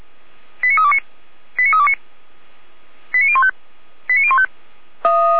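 Five-tone selective call (ZVEI), heard as quick runs of five tones, tones of about 70 ms. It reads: 89329 89329 89015 89015
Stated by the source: Five-tone ZVEI selective call received with an SDR.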